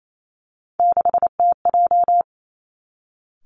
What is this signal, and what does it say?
Morse code, heard as a keyed tone, about 28 wpm, 700 Hz. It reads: 6TJ
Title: Three Challenging Letter-Number Combinations 28wpm